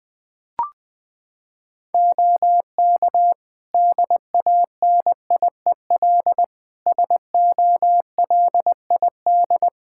Morse code, heard as a keyed tone, about 20 wpm, 700 Hz. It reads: OK DANIEL SOLID
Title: K1USN SST QSO Practice 20wpm